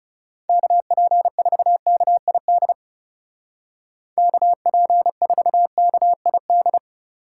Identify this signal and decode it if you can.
Morse code, heard as a keyed tone, about 38 wpm, 700 Hz. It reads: KP4KID KP4KID